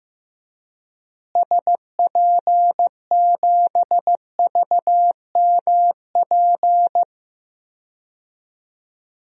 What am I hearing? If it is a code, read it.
Morse code, heard as a keyed tone, about 15 wpm, 700 Hz. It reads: SP7VMP
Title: DXCC-100 Call Signs - 6 Characters - Repeated 3x - 15wpm